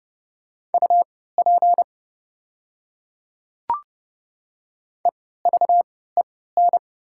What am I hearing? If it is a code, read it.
Morse code, heard as a keyed tone, about 30 wpm, spaced as for 17 wpm, 700 Hz. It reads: UP EVEN
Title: 3x Character Spacing - Top 500 Words - 30wpm